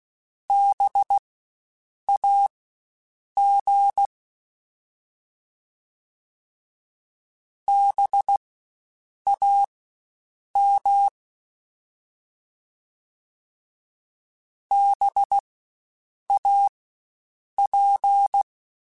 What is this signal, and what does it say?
Morse code, heard as a keyed tone, about 16 wpm, spaced as for 7 wpm, 800 Hz. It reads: BAG BAM BAP
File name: Spaced Alphas / Spaced Alphas mp3s / Bs sp @15